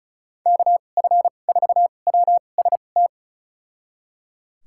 Morse code, heard as a keyed tone, about 36 wpm, spaced as for 25 wpm, 700 Hz. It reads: KF4WST
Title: USA Call Signs - 6 Characters - 2x Character Spacing - 35wpm